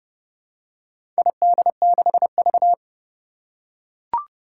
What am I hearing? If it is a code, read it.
Morse code, heard as a keyed tone, about 30 wpm, 700 Hz. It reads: ID6V
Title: Four Challenging Letter-Number Combinations 30wpm